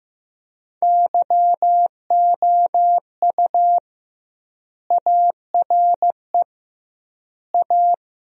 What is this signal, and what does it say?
Morse code, heard as a keyed tone, about 15 wpm, 700 Hz. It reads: YOU ARE A
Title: 2x Word-Spacing - Sentences from Top 400 Words - 15wpm